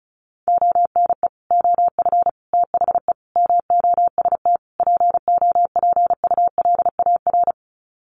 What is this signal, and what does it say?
Morse code, heard as a keyed tone, about 34 wpm, 700 Hz. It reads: ONE OF THE MOST POPULAR
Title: Sets of 5 Words 35wpm